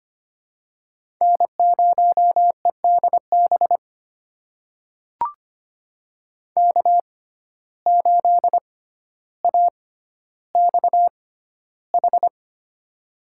Morse code, heard as a keyed tone, about 25 wpm, spaced as for 13 wpm, 700 Hz. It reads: N0EDB K8AXH
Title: USA Call Signs - 5 Characters - 6x Character Spacing - 25wpm